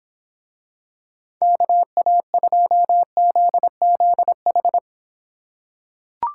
Morse code, heard as a keyed tone, about 24 wpm, 700 Hz. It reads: KA2ZZH